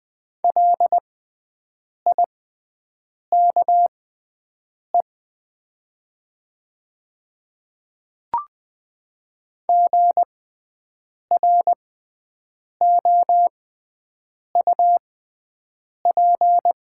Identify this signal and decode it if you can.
Morse code, heard as a keyed tone, about 20 wpm, spaced as for 7 wpm, 700 Hz. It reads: LIKE GROUP
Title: 6x Character Spacing - Top 300 Words - 20wpm